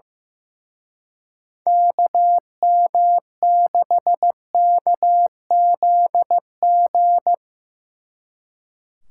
Morse code, heard as a keyed tone, about 17 wpm, 700 Hz. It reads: KM6KZG
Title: Speed Racing - USA 2x3 Call Signs 15wpm